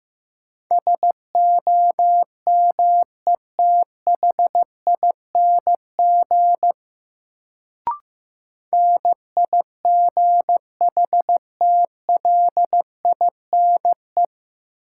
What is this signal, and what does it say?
Morse code, heard as a keyed tone, about 15 wpm, 700 Hz. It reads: SOMETHING NIGHTLINE
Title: Sets of 5 Words - New Word Drill - 15wpm